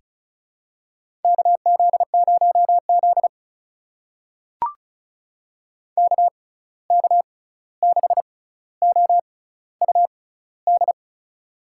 Morse code, heard as a keyed tone, about 36 wpm, spaced as for 16 wpm, 700 Hz. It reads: KZ0Z KK6OUD
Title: USA Call Signs - 6x Character Spacing - 35wpm